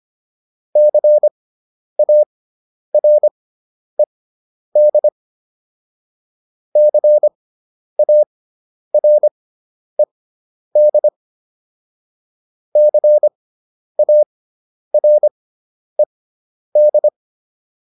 Morse code, heard as a keyed tone, about 25 wpm, spaced as for 10 wpm, 600 Hz. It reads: CARED CARED CARED
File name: ED5 10